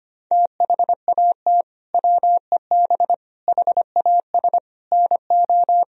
Morse code, heard as a keyed tone, about 25 wpm, 700 Hz. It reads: THAT WEB HAS NO
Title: Sentences from Top 1000 Words - Review Part 4 - 25wpm